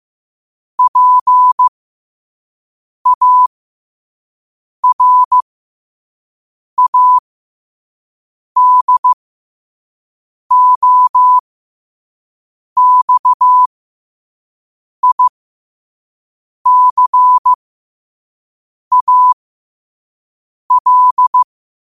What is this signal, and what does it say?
Morse code, heard as a keyed tone, about 15 wpm, spaced as for 5 wpm, 1000 Hz. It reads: PARADOXICAL
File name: qotdfiles / 5wpm